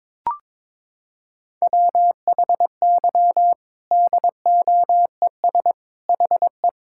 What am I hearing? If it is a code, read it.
Morse code, heard as a keyed tone, about 22 wpm, 700 Hz. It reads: WHY DOES HE
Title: Sentences from Top 1600 Words 22wpm